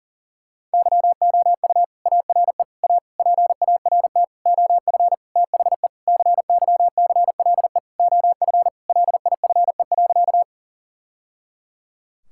Morse code, heard as a keyed tone, about 40 wpm, 700 Hz. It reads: YOU ARE A PART OF THE CYCLE OF LIFE.
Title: Sentences from Top 1000 Words - Review Part 6 - 40wpm